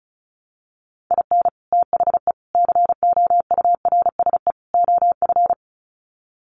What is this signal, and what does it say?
Morse code, heard as a keyed tone, about 34 wpm, 700 Hz. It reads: IN THE COURSE OF
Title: Sets of 4 Words 35wpm